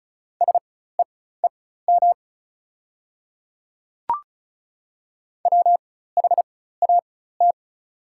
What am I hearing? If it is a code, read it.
Morse code, heard as a keyed tone, about 34 wpm, spaced as for 16 wpm, 690 Hz. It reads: SEEM WHAT
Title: 4x Character Spacing - Top 300 Words - 35wpm